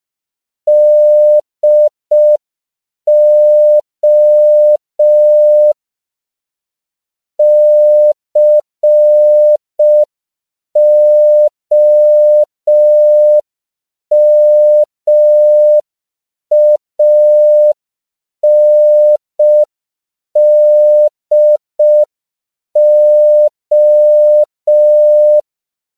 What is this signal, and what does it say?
Morse code, heard as a keyed tone, about 5 wpm, 600 Hz. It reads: DO COMANDO
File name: CW / 5PPM